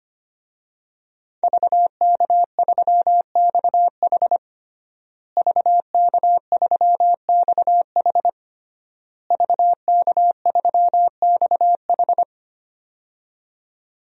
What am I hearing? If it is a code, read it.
Morse code, heard as a keyed tone, about 25 wpm, 700 Hz. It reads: VK3XH VK3XH VK3XH
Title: DXCC-100 Call Signs - 5 Characters - Repeated 3x - 25wpm